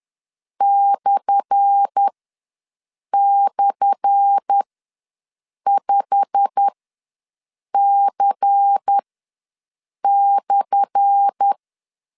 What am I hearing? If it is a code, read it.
Morse code, heard as a keyed tone, about 10 wpm, spaced as for 6 wpm, 800 Hz. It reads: //5C/
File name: LEZIONI / 10 SIGNS / 10A SLASH ws 10wpm 4121